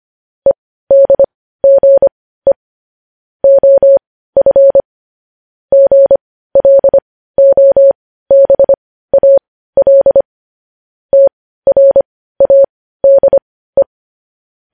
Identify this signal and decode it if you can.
Morse code, heard as a keyed tone, about 25 wpm, spaced as for 15 wpm, 550 Hz. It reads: EDGE OF GLOBAL TRADE
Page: news-headlines-in-morse-code-at-15-wpm